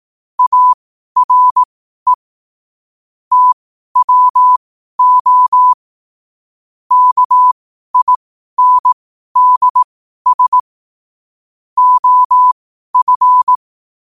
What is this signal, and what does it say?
Morse code, heard as a keyed tone, about 18 wpm, spaced as for 12 wpm, 1000 Hz. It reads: ARE TWO KINDS OF
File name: 12wpm